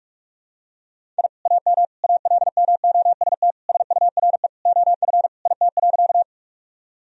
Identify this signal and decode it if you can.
Morse code, heard as a keyed tone, about 46 wpm, 700 Hz. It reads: I AM ALMOST SURE OF IT.
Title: Sentences from Top 1000 Words - Review Part 4 - 45wpm